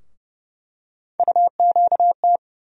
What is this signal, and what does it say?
Morse code, heard as a keyed tone, about 30 wpm, 700 Hz. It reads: UQT
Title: USA Call Sign Postfixes - 3 Letters 30wpm